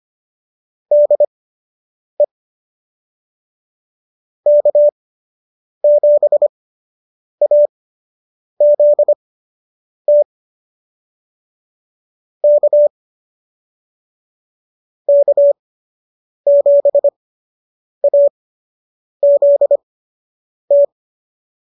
Morse code, heard as a keyed tone, about 25 wpm, spaced as for 8 wpm, 590 Hz. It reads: DE K7AZT K K7AZT
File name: qso006 08